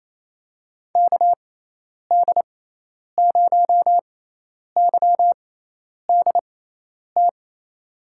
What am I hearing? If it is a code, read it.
Morse code, heard as a keyed tone, about 28 wpm, spaced as for 10 wpm, 700 Hz. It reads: KD0YDT
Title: USA Call Signs - 6 Characters - 6x Character Spacing - 28wpm